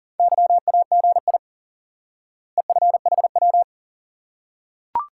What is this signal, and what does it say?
Morse code, heard as a keyed tone, about 40 wpm, 700 Hz. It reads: YAGI EFHW
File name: IQE Course 3 - Lesson 89 - YAGI - Two Elements - 40wpm